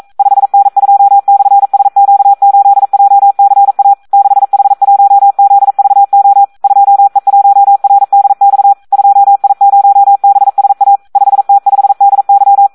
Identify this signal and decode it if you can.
Morse code, heard as a keyed tone, about 42 wpm, 800 Hz, with radio noise. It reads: 5N2/SQ8JCA 6H1ZVO 2E1RDX 2I0BSA 5T5DY
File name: Audio practica CW / SD 1 42